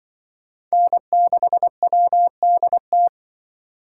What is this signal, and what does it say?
Morse code, heard as a keyed tone, about 25 wpm, 700 Hz. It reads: N6WDT